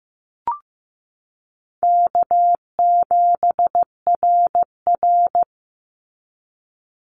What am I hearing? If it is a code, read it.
Morse code, heard as a keyed tone, about 15 wpm, 700 Hz. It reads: K7RR
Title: Call Signs Repeated 3x - 15wpm